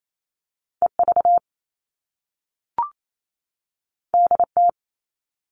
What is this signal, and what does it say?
Morse code, heard as a keyed tone, about 28 wpm, 700 Hz. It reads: EV DT
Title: All Cut Numbers - 2-Digit 28wpm